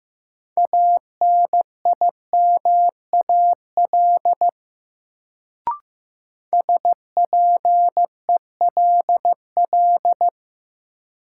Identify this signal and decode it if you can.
Morse code, heard as a keyed tone, about 15 wpm, 700 Hz. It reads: ANIMAL SPELL
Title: Sentences from Top 1000 Words - Review Part 1 - New Word Drill - 15wpm